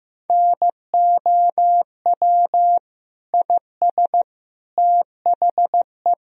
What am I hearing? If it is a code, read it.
Morse code, heard as a keyed tone, about 15 wpm, 700 Hz. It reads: NOW IS THE
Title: Sentences from Top 1000 Words - Review Part 4 - 15wpm